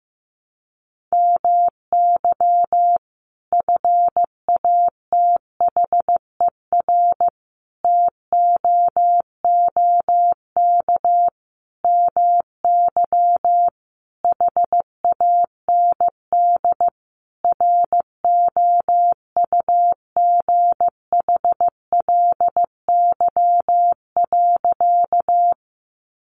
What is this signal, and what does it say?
Morse code, heard as a keyed tone, about 15 wpm, 700 Hz. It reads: MY FATHER TOOK MY HAND ROUGHLY.
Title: Sentences from Top 1500 Words 15wpm